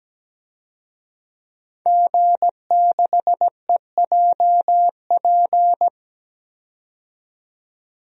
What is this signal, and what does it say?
Morse code, heard as a keyed tone, about 17 wpm, 700 Hz. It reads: G6EJP
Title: DXCC-100 Call Signs - 5 Characters - 17wpm